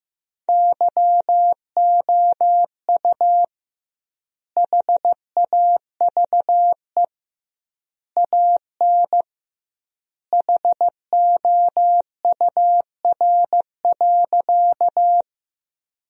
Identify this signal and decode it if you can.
Morse code, heard as a keyed tone, about 15 wpm, 700 Hz. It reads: YOU HAVE AN HOUR.
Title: 2x Word-Spacing - Sentences from Top 400 Words - 15wpm